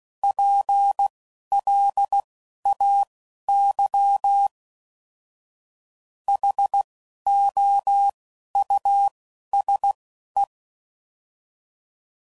Morse code, heard as a keyed tone, about 16 wpm, spaced as for 11 wpm, 800 Hz. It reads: PLAY HOUSE
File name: Suffixes / General / Houses